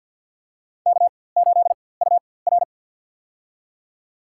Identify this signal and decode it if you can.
Morse code, heard as a keyed tone, about 51 wpm, spaced as for 23 wpm, 700 Hz. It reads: K8UR
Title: USA Call Signs - 4x Character Spacing - 50wpm